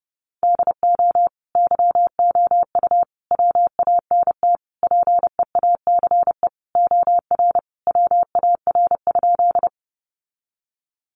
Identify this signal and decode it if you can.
Morse code, heard as a keyed tone, about 28 wpm, 700 Hz. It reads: DO YOU WANT PEACE OR WAR?
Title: Sentences from Top 1500 Words 30wpm